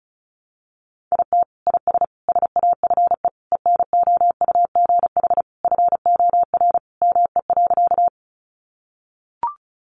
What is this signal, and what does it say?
Morse code, heard as a keyed tone, about 34 wpm, 700 Hz. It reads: IT IS SAFE ENOUGH FOR ME.
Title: Sentences from Top 900 Words 35wpm